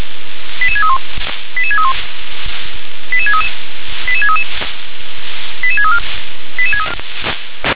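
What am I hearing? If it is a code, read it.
Five-tone selective call (ZVEI), heard as quick runs of five tones, tones of about 70 ms. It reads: 88631 88631 88633 88633 88634
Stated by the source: ZVEI five-tone selective call received on SDR.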